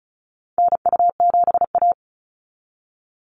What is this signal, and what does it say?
Morse code, heard as a keyed tone, about 34 wpm, 700 Hz. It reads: NU7A
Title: NA SOTA Chaser Callsigns 35wpm